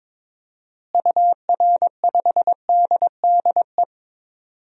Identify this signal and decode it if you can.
Morse code, heard as a keyed tone, about 22 wpm, 700 Hz. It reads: UR5DDE
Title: DXCC-100 Call Signs - 6 Characters - Repeated 3x - 22wpm